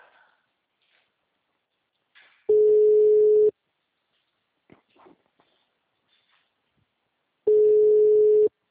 Ringback tone (Europe, 425 Hz).
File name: out- -2001- - - .15224